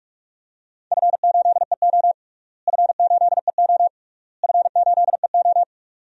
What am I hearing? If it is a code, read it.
Morse code, heard as a keyed tone, about 45 wpm, 700 Hz. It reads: F8EO F8EO F8EO
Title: DXCC-100 Call Signs - 4 Characters - Repeated 3x - 45wpm